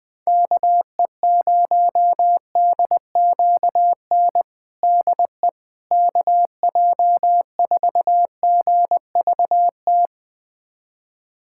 Morse code, heard as a keyed tone, about 20 wpm, 700 Hz. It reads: KE0DQN DE KJ4GVT
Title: USA Callsign DE Callsign - 6 Characters - 20wpm